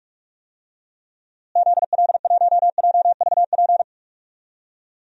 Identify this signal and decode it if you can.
Morse code, heard as a keyed tone, about 41 wpm, 700 Hz. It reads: ZL1JVP
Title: DXCC-100 Call Signs - 6 Characters - Repeated 3x - 45wpm